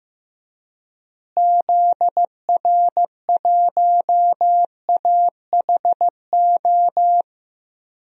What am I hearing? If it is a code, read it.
Morse code, heard as a keyed tone, about 16 wpm, 700 Hz. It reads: ZR1AHO
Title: DXCC-100 Call Signs - 6 Characters - Repeated 2x - 15wpm